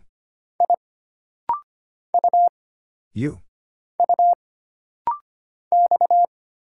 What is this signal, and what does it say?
Morse code, heard as a keyed tone, about 25 wpm, 700 Hz. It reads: I U U X
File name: Single Character (Letter:Number:Punctuation:Prosign) - Rapid-Fire - 25wpm